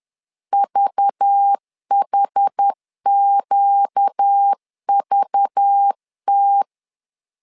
Morse code, heard as a keyed tone, about 10 wpm, 800 Hz. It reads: VHQVT
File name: LEZIONI / 5 LBVC / 5 LBV 10wpm 2248